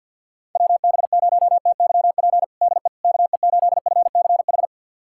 Morse code, heard as a keyed tone, about 51 wpm, 690 Hz. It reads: WB0TYP DE KE8FCH
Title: USA Callsign DE Callsign - 6 Characters - 50wpm